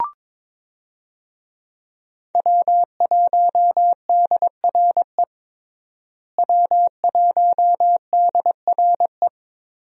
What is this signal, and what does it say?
Morse code, heard as a keyed tone, about 22 wpm, 700 Hz. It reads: W1DRE W1DRE
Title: USA Call Signs - 5 Characters - Repeated 2x - 22wpm